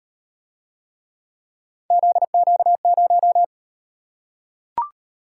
Morse code, heard as a keyed tone, about 38 wpm, 700 Hz. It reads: ZQ0